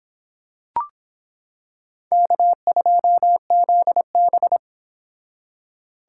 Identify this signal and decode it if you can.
Morse code, heard as a keyed tone, about 26 wpm, 700 Hz. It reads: K2ZB